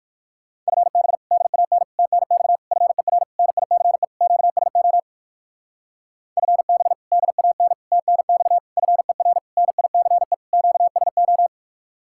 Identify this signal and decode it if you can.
Morse code, heard as a keyed tone, about 60 wpm, 700 Hz. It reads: FB DAN TNX FER NICE QSO FB DAN TNX FER NICE QSO